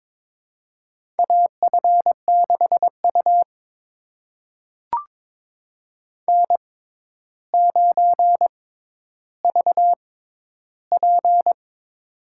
Morse code, heard as a keyed tone, about 22 wpm, spaced as for 11 wpm, 700 Hz. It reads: AF6U N9VP
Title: USA Call Signs - 4 Characters - 6x Character Spacing - 22wpm